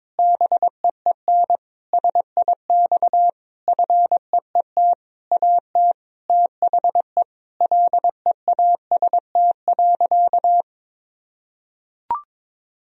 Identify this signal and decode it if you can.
Morse code, heard as a keyed tone, about 22 wpm, 700 Hz. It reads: BEEN SIX FEET AT THE LEAST.
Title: Sentences from Top 1000 Words - Review Part 2 - 22wpm